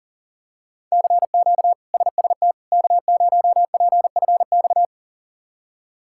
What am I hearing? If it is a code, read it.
Morse code, heard as a keyed tone, about 40 wpm, 700 Hz. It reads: CQ SST K0PFX
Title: K1USN SST QSO Practice - 40wpm